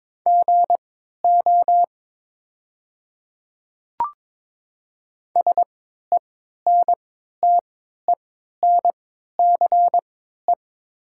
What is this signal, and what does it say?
Morse code, heard as a keyed tone, about 22 wpm, spaced as for 12 wpm, 700 Hz. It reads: GO SENTENCE
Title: 3x Character Spacing - Top 300 Words - 22wpm